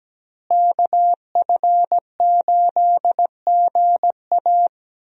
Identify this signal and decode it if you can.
Morse code, heard as a keyed tone, about 17 wpm, 700 Hz. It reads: KF8GA
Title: USA Call Signs - 5 Characters - Repeated 2x - 17wpm